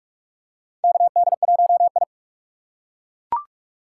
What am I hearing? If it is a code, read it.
Morse code, heard as a keyed tone, about 42 wpm, 700 Hz. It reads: KD1I
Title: USA Call Signs - 4x Character Spacing - 45wpm